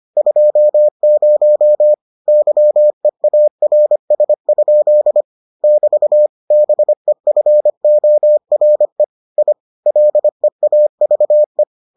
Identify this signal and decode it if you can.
Morse code, heard as a keyed tone, about 25 wpm, 600 Hz. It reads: 20 YEARS? <BT> BEFORE I LEAVE